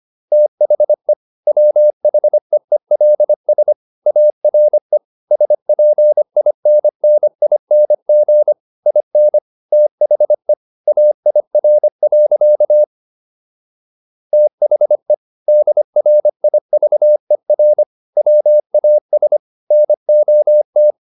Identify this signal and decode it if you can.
Morse code, heard as a keyed tone, about 25 wpm, 600 Hz. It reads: THE WHEELS ARE SPINNING IN THE AIR. THE DRIVER WAS NOT